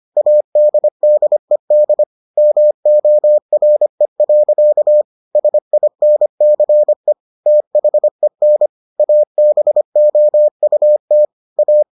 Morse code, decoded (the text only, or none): ADDED MORE. SINCE THEN ABOUT A